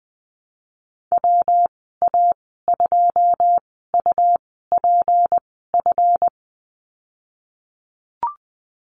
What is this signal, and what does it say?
Morse code, heard as a keyed tone, about 20 wpm, spaced as for 15 wpm, 700 Hz. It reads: WA2UPF
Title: 186 Lesson 30g - Letter b - Call Signs 20wpm